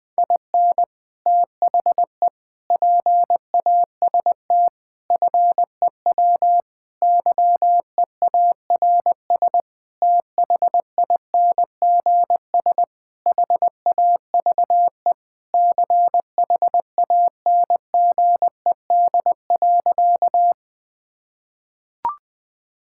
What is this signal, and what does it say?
Morse code, heard as a keyed tone, about 20 wpm, 700 Hz. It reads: IN THE PAST FEW YEARS THINGS HAVE CHANGED.
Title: Sentences with 5-Word Phrases - 20wpm